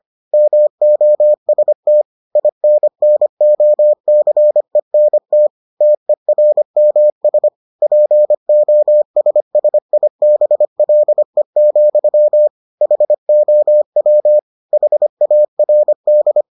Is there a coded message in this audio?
Yes: Morse code from 0.3 to 16.4 s, heard as a keyed tone, about 25 wpm, 600 Hz. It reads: MOST INNOCENT TERMS POSSIBLE, HOW HARD